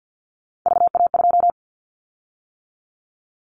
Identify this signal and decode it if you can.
Morse code, heard as a keyed tone, about 50 wpm, 700 Hz. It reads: VA2
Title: Callsign Prefixes for Top 50 DXCC - 50wpm